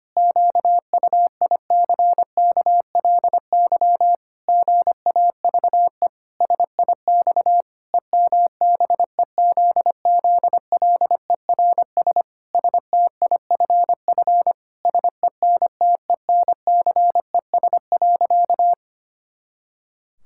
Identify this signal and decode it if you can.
Morse code, heard as a keyed tone, about 25 wpm, 700 Hz. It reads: QUICKLY GAVE SIX EMBEZZLERS STIFF SENTENCES.
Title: Pangram Challenge v2 - 25wpm